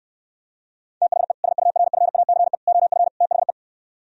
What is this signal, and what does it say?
Morse code, heard as a keyed tone, about 67 wpm, 700 Hz. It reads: THE SURFACE OF THE